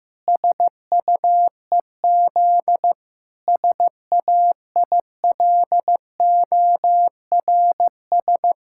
Morse code, heard as a keyed tone, about 15 wpm, 700 Hz. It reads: SUEZ SAILORS
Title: Pangram Challenge v2 - 15wpm